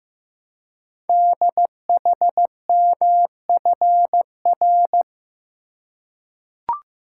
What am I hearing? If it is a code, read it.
Morse code, heard as a keyed tone, about 15 wpm, 690 Hz. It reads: DHMFR
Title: Code Groups - 5 Letters - 15wpm